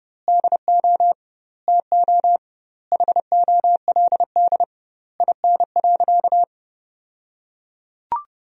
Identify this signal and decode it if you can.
Morse code, heard as a keyed tone, about 30 wpm, 700 Hz. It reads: DO TO HOLD IN.
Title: 2x Word-Spacing - Sentences from Top 400 Words - 30wpm